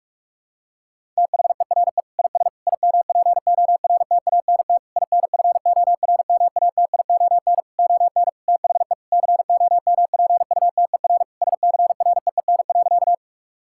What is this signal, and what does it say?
Morse code, heard as a keyed tone, about 46 wpm, 700 Hz. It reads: THERE IS IMPORTANT INFORMATION ON THE COMPUTER SCREEN.
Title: Sentences from Top 1000 Words - Review Part 6 - 45wpm